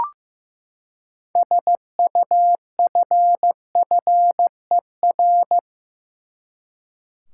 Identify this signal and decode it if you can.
Morse code, heard as a keyed tone, about 15 wpm, 700 Hz. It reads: SUFFER
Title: Six Letter Words - 15wpm